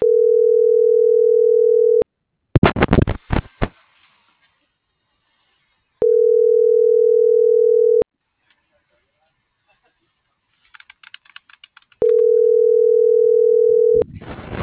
Ringback tone (North America), 440 + 480 Hz.